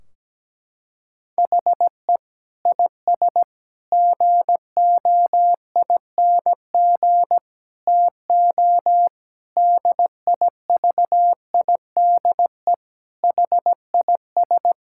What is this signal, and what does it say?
Morse code, heard as a keyed tone, about 17 wpm, 700 Hz. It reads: HE IS GOING TO DIVIDE HIS
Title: Sentences from Top 1600 Words 17wpm